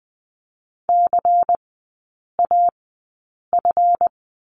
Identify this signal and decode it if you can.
Morse code, heard as a keyed tone, about 20 wpm, spaced as for 8 wpm, 700 Hz. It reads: CAF
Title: 152 Lesson 25e - Letter p - Three Characters 20wpm